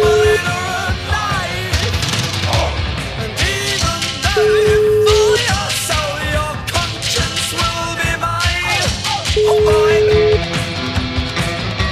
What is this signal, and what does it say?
Ringback tone (Europe, 425 Hz), with noise.